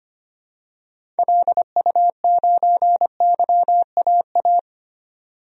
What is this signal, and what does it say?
Morse code, heard as a keyed tone, about 25 wpm, 700 Hz. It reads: LU9YAA
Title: DXCC-100 Call Signs - 6 Characters - 25wpm